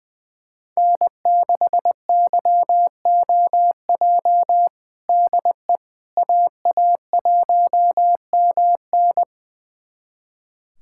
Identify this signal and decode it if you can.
Morse code, heard as a keyed tone, about 20 wpm, 700 Hz. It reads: N6YOJ DE AA1MN
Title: USA Callsign DE Callsign - 5 Characters - 20wpm